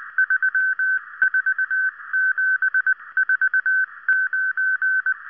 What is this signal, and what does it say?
Morse code, heard as a keyed tone, about 19 wpm, 1550 Hz. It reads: 34749